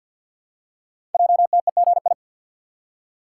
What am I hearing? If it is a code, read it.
Morse code, heard as a keyed tone, about 52 wpm, 700 Hz. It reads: JTEGI
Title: Code Groups - 5 Letters - 50wpm